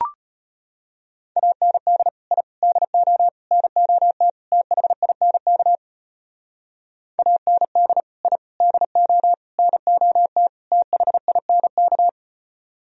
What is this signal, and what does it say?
Morse code, heard as a keyed tone, about 30 wpm, 700 Hz. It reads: AND I DO NOT THINK AND I DO NOT THINK
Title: Speed-Racing - Sets of 5 Words 25wpm